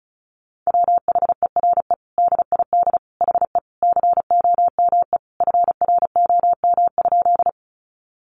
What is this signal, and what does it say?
Morse code, heard as a keyed tone, about 34 wpm, 700 Hz. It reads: WHERE DID HE COME FROM?
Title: Sentences from Top 200 Words 35wpm